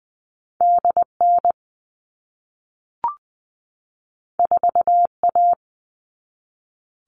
Morse code, heard as a keyed tone, about 20 wpm, 700 Hz. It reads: DN 4A
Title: All Cut Numbers - 2-Digit 20wpm